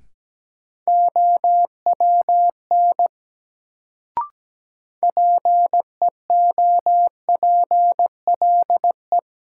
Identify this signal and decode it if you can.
Morse code, heard as a keyed tone, about 17 wpm, 700 Hz. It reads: OWN PEOPLE
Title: Sets of 2 Words - New Word Drill - 17wpm